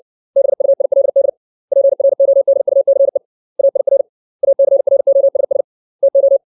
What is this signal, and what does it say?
Morse code, heard as a keyed tone, about 60 wpm, 550 Hz. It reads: BUILD PRODUCE AIR ACROSS TO
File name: 632 words / 632 Words 60WPM